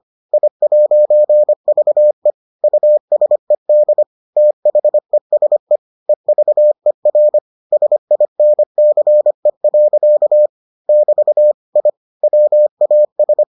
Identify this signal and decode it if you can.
Morse code, heard as a keyed tone, about 25 wpm, 600 Hz. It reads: I'VE USED THESE EVER SINCE. <BT> I WAS